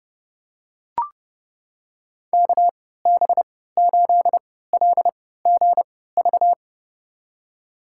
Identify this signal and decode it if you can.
Morse code, heard as a keyed tone, about 30 wpm, spaced as for 17 wpm, 700 Hz. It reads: KB8LGV
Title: USA Call Signs - 6 Characters - 3x Character Spacing - 30wpm